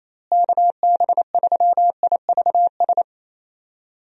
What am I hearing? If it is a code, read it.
Morse code, heard as a keyed tone, about 28 wpm, 700 Hz. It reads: KB3IVS
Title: USA Call Signs - 6 Characters - Repeated 2x - 28wpm